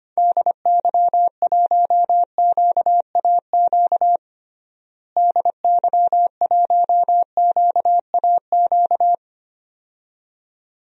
Morse code, heard as a keyed tone, about 25 wpm, 700 Hz. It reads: DY1QAQ DY1QAQ
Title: DXCC-100 Call Signs - 6 Characters - Repeated 2x - 25wpm